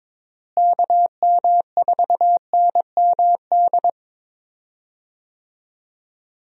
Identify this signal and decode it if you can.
Morse code, heard as a keyed tone, about 22 wpm, 700 Hz. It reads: KM4NMD
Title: USA Call Signs - 6 Characters - Repeated 2x - 22wpm